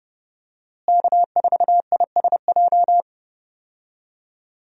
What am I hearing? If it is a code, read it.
Morse code, heard as a keyed tone, about 32 wpm, 700 Hz. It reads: K4ISJ
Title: USA Call Signs - 5 Characters - Repeated 2x - 30wpm